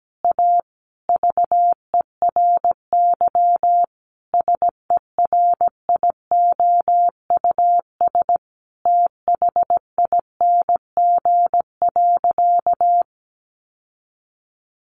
Morse code, heard as a keyed tone, about 17 wpm, 700 Hz. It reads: A VERY SERIOUS THING.
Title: Sentences from Top 1500 Words 17wpm